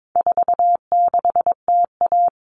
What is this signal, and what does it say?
Morse code, heard as a keyed tone, about 22 wpm, 700 Hz. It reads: 46TA
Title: All Cut Numbers - 4-Digit 22wpm